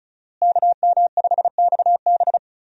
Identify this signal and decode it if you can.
Morse code, heard as a keyed tone, about 36 wpm, 700 Hz. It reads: KM5XB
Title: USA Call Signs - 5 Characters - Repeated 3x - 35wpm